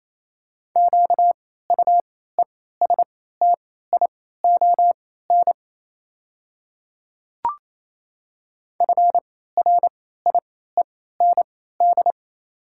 Morse code, heard as a keyed tone, about 28 wpm, spaced as for 16 wpm, 700 Hz. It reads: QUESTION FRIEND
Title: 3x Character Spacing - Top 500 Words - 28wpm